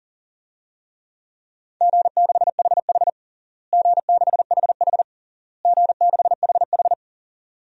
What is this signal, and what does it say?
Morse code, heard as a keyed tone, about 37 wpm, 700 Hz. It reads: G6HH G6HH G6HH
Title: DXCC-100 Call Signs - 4 Characters - Repeated 3x - 40wpm